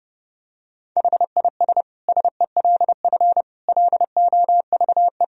Morse code, heard as a keyed tone, about 30 wpm, 700 Hz. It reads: HIS SELF LOVE
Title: Sentences from Top 1000 Words - Review Part 1 - 30wpm